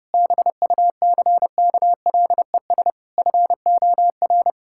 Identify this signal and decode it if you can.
Morse code, heard as a keyed tone, about 30 wpm, 700 Hz. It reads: BUCKLES FOR
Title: Pangram Challenge v2 - 30wpm